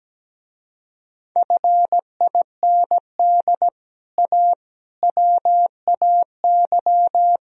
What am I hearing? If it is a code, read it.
Morse code, heard as a keyed tone, about 18 wpm, 700 Hz. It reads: FIND A WAY
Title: Speed-Racing - Sets of 4 Words 17wpm